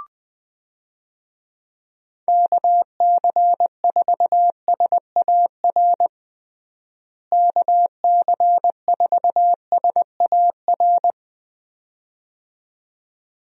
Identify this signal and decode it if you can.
Morse code, heard as a keyed tone, about 20 wpm, 700 Hz. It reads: KC4SAR KC4SAR
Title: USA Call Signs - 6 Characters - Repeated 2x - 20wpm